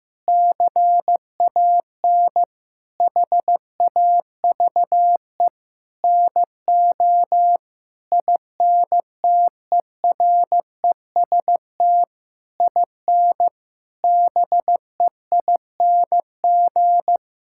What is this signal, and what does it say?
Morse code, heard as a keyed tone, about 15 wpm, 700 Hz. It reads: CAN HAVE NO INTEREST IN BEING
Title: Sentences from Top 1000 Words - Review Part 6 - 15wpm